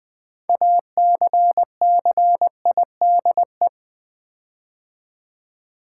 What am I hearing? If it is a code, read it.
Morse code, heard as a keyed tone, about 20 wpm, 700 Hz. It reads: ACCIDE
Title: Build A Word - 6 Letters At A Time - 12-Letter Words - 20wpm